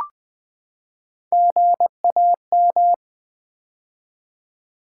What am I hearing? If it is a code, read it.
Morse code, heard as a keyed tone, about 20 wpm, 700 Hz. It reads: GAM
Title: Build A Word - 3 Letters At A Time - 6-Letter Words - 20wpm